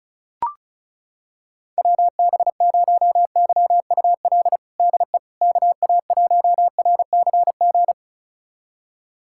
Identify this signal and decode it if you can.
Morse code, heard as a keyed tone, about 36 wpm, 690 Hz. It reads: WB0YUL DE KA1RCG
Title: USA Callsign DE Callsign 35wpm